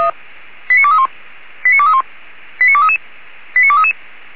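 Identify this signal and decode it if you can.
Five-tone selective call (ZVEI), heard as quick runs of five tones, tones of about 80 ms. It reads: 89231 89231 89233 89233
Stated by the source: ZVEI five-tone selective call received via SDR, code 89241.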